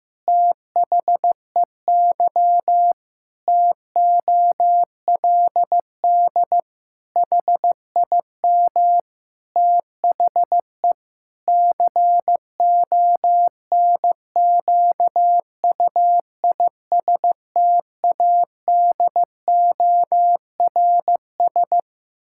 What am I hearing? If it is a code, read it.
Morse code, heard as a keyed tone, about 15 wpm, 700 Hz. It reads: THEY TOLD HIM THE CONQUISTADORS
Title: Pangram Challenge v2 - 15wpm